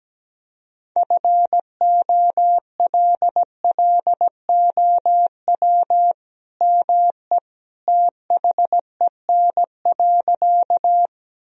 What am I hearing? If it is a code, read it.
Morse code, heard as a keyed tone, about 17 wpm, 700 Hz. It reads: FOLLOW ME THEN.
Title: Sentences from Top 1000 Words - Review Part 1 - 17wpm